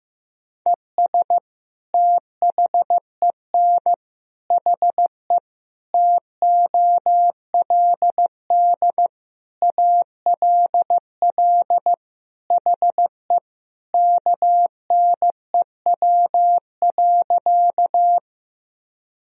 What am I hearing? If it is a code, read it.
Morse code, heard as a keyed tone, about 16 wpm, 700 Hz. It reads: ES THEN HE TOLD ALL HE KNEW.
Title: Sentences from Top 400 Words and Abbreviations 15wpm